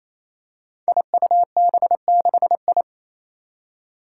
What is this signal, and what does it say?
Morse code, heard as a keyed tone, about 28 wpm, 700 Hz. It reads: IUB6I
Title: Five Challenging Letter-Number Combinations 28wpm